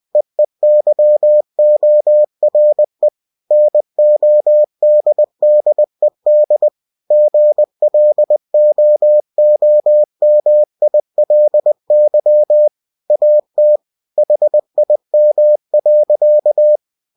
Morse code, heard as a keyed tone, about 20 wpm, 600 Hz. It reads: EEYORE NODDED GLOOMILY AT HIM.
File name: cw-books / winnie-the-pooh / winnie-20wpm-page-0106